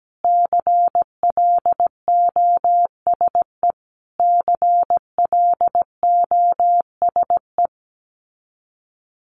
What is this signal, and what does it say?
Morse code, heard as a keyed tone, about 17 wpm, 700 Hz. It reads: CLOSE CLOSE
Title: Top 500 Words Repeated 17wpm